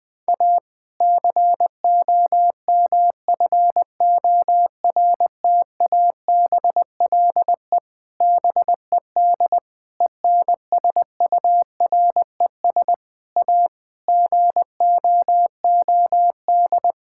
Morse code, heard as a keyed tone, about 21 wpm, 700 Hz. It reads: A COMFORTABLE BED ENSURES A GOOD
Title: Difficult Sentences with Hints - 20wpm